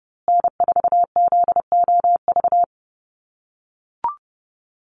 Morse code, heard as a keyed tone, about 30 wpm, 700 Hz. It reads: N4ZOV
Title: USA Call Signs Repeated 3x 30wpm